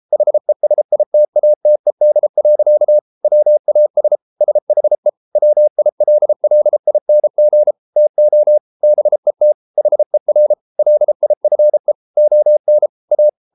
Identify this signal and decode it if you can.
Morse code, heard as a keyed tone, about 33 wpm, 600 Hz. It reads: HESITATED. WAS SHE WILLING TO BET HER LIFE ON A